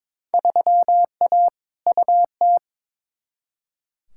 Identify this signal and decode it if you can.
Morse code, heard as a keyed tone, about 22 wpm, 700 Hz. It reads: 3A UT
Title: Field Day Exchanges - 22wpm